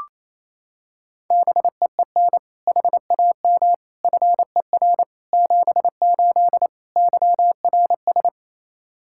Morse code, heard as a keyed tone, about 28 wpm, 700 Hz. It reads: BEEN HAM FER 78 YRS
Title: Practice QSO - Exchange 3 - More Info - 28wpm